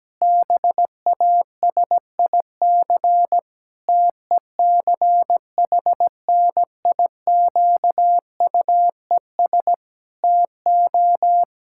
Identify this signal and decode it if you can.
Morse code, heard as a keyed tone, about 17 wpm, 700 Hz. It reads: BASIC TECHNIQUES TO
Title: Pangram Challenge v2 - 17wpm